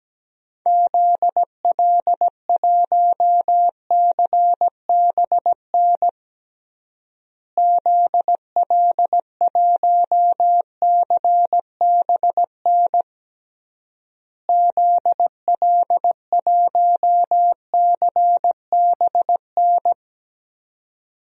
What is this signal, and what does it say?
Morse code, heard as a keyed tone, about 17 wpm, 700 Hz. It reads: ZL1CBN ZL1CBN ZL1CBN
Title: DXCC-100 Call Signs - 6 Characters - Repeated 3x - 17wpm